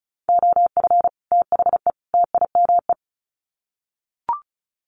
Morse code, heard as a keyed tone, about 34 wpm, 700 Hz. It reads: OF THE TIME
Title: Sets of 3 Words 35wpm